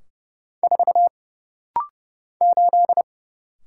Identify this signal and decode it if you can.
Morse code, heard as a keyed tone, about 30 wpm, 700 Hz. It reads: 4 8
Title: Single Letter-Number - Rapid-Fire 30wpm